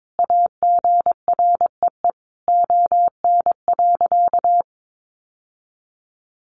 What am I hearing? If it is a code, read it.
Morse code, heard as a keyed tone, about 22 wpm, 700 Hz. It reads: AGREE ON.
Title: Sentences from Top 1100 Words 22wpm